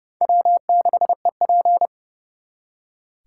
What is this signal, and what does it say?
Morse code, heard as a keyed tone, about 36 wpm, 700 Hz. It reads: W6EP
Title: Speed Racing - USA 1x2 Call Signs 30wpm